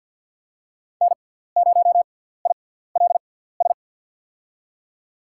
Morse code, heard as a keyed tone, about 47 wpm, spaced as for 17 wpm, 700 Hz. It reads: N0ILS
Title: USA Call Signs - 5 Characters - 6x Character Spacing - 50wpm